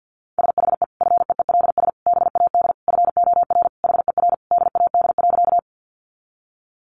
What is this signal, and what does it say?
Morse code, heard as a keyed tone, about 50 wpm, 700 Hz. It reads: SHE FEELS BAD FOR HER DAD.
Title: Additional Easy Phrases 50wpm